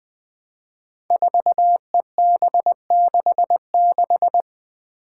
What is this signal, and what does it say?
Morse code, heard as a keyed tone, about 20 wpm, 700 Hz. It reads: 4EB66
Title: Five Challenging Letter-Number Combinations 20wpm